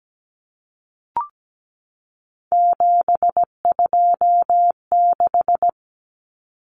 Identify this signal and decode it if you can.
Morse code, heard as a keyed tone, about 17 wpm, 700 Hz. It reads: 726
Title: Numbers - 3-Digit 17wpm